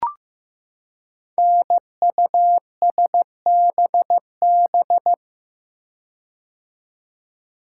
Morse code, heard as a keyed tone, about 15 wpm, 700 Hz. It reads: NUSBB
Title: Code Groups - 5 Letters - 15wpm